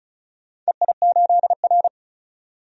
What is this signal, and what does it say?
Morse code, heard as a keyed tone, about 36 wpm, 700 Hz. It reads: EI8R
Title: DXCC-100 Call Signs - 4 Characters - 35wpm